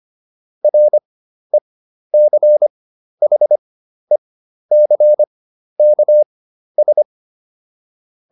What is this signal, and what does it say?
Morse code, heard as a keyed tone, about 25 wpm, spaced as for 12 wpm, 600 Hz. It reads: RECHECKS